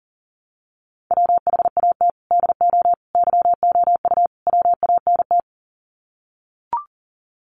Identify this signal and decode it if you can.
Morse code, heard as a keyed tone, about 40 wpm, 700 Hz. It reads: WHAT DO YOU WANT
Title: Sets of 4 Words 40wpm